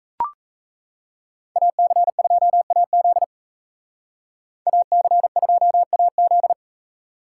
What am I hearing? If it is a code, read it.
Morse code, heard as a keyed tone, about 36 wpm, 700 Hz. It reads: AC2AZ AC2AZ